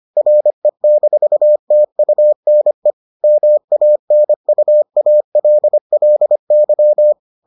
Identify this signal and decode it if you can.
Morse code, heard as a keyed tone, about 25 wpm, 600 Hz. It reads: RE-TUNE MANUALLY